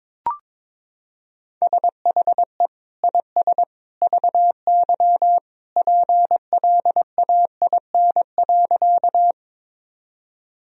Morse code, heard as a keyed tone, about 22 wpm, 700 Hz. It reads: SHE IS VY PLAIN.
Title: Sentences from Top 400 Words and Abbreviations 22wpm